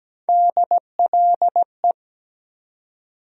Morse code, heard as a keyed tone, about 17 wpm, 700 Hz. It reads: DLE
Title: USA Call Sign Postfixes - 3 Letters 17wpm